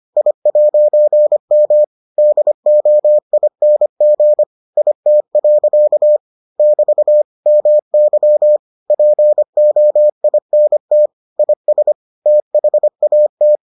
Morse code, heard as a keyed tone, about 25 wpm, 600 Hz. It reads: I'M DOING IT. <BT> MY POINT IS THAT